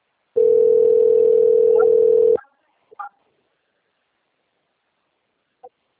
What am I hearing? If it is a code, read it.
Ringback tone (North America), 440 + 480 Hz.
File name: out- - - - - .9490